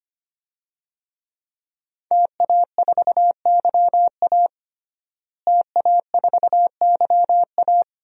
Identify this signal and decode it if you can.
Morse code, heard as a keyed tone, about 25 wpm, 700 Hz. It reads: TA4YA TA4YA
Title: DXCC-100 Call Signs - 5 Characters - Repeated 3x - 25wpm